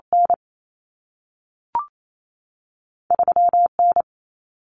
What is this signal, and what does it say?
Morse code, heard as a keyed tone, about 28 wpm, 700 Hz. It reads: N 3 N
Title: Common Cut Numbers - 2-Digit 28wpm